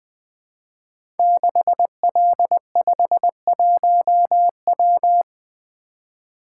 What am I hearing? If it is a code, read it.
Morse code, heard as a keyed tone, about 20 wpm, 700 Hz. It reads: 6L51W
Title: Code Groups - 5 Letter:Numbers - 20wpm